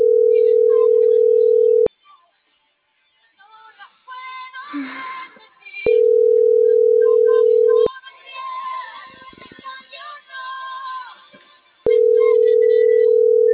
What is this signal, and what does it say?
Ringback tone (North America), 440 + 480 Hz.
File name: CENTRAL NMP / out melanie.diaznmp